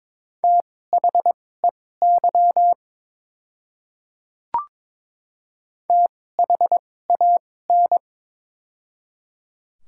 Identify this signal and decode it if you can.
Morse code, heard as a keyed tone, about 21 wpm, spaced as for 16 wpm, 700 Hz. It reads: THEY THAN
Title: 2x Character Spacing - Top 100 Words - 22wpm